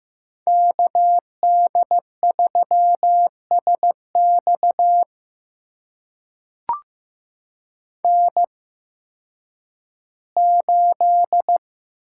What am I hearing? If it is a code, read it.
Morse code, heard as a keyed tone, about 15 wpm, spaced as for 10 wpm, 700 Hz. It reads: KD3SX N8
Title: USA Call Signs - 5 Characters - 8x Character Spacing - 15wpm